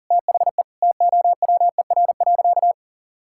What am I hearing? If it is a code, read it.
Morse code, heard as a keyed tone, about 40 wpm, 700 Hz. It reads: THE TOWER.
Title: Pangram Challenge v2 - 40wpm